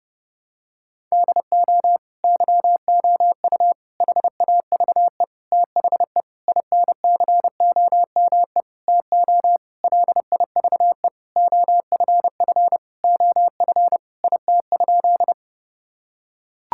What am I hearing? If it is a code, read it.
Morse code, heard as a keyed tone, about 30 wpm, 700 Hz. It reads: DO YOU HAVE THE INCOME TO LIVE OFF OF IT?
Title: Sentences from Top 1000 Words - Review Part 5 - 30wpm